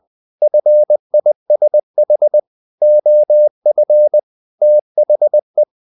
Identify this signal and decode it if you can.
Morse code, heard as a keyed tone, about 20 wpm, 600 Hz. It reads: FISH OF THE
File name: the old man and the sea part 2 20wpm